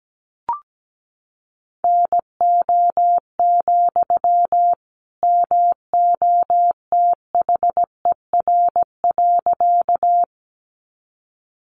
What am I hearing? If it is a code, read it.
Morse code, heard as a keyed tone, about 17 wpm, 700 Hz. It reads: NO, MOTHER.
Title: Sentences from Top 200 Words 17wpm